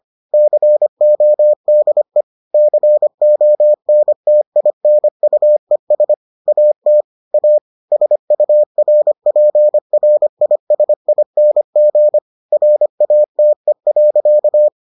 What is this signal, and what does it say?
Morse code, heard as a keyed tone, about 25 wpm, 600 Hz. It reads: CODE CONTINUES AT A SURPRISING RATE.